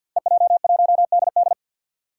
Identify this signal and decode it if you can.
Morse code, heard as a keyed tone, about 51 wpm, 700 Hz. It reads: EJ1DD
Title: Five Challenging Letter-Number Combinations 50wpm